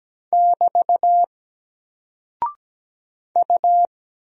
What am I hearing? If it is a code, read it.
Morse code, heard as a keyed tone, about 17 wpm, 700 Hz. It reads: <BT> U
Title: Single Character (Letter:Number:Punctuation:Prosign) Alt 17wpm